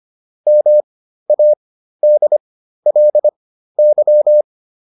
Morse code, heard as a keyed tone, about 25 wpm, spaced as for 13 wpm, 600 Hz. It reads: MADLY
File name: LY5 13